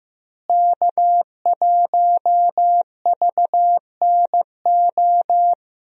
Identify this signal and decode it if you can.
Morse code, heard as a keyed tone, about 15 wpm, 700 Hz. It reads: K1VNO
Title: USA Call Signs - 5 Characters - Repeated 2x - 15wpm